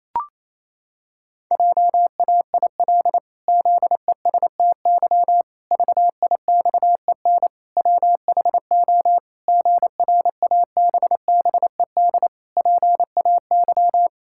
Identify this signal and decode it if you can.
Morse code, heard as a keyed tone, about 28 wpm, 700 Hz. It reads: JAIL ZESTY VIXEN WHO GRABBED PAY
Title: Pangram Challenge v2 - 28wpm